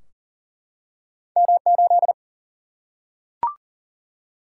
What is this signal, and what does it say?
Morse code, heard as a keyed tone, about 40 wpm, 700 Hz. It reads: M8
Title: Single-Letter Single-Number 40wpm